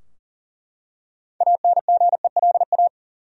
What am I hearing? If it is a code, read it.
Morse code, heard as a keyed tone, about 40 wpm, 700 Hz. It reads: ANGELA
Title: USA Common Names - 40wpm